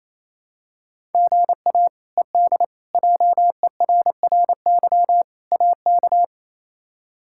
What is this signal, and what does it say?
Morse code, heard as a keyed tone, about 27 wpm, 700 Hz. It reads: GA ED JERRY AK
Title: K1USN SST QSO Practice - 28wpm